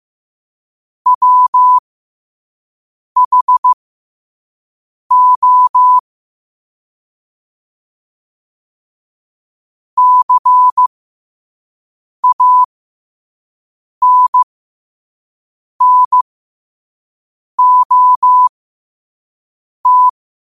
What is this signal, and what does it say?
Morse code, heard as a keyed tone, about 15 wpm, spaced as for 5 wpm, 1000 Hz. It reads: WHO CANNOT
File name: qotdfiles / 5wpm